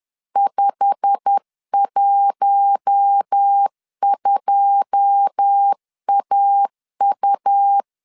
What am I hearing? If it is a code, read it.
Morse code, heard as a keyed tone, about 10 wpm, 800 Hz. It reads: 512AU
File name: LEZIONI / 9 2 10wpm 2736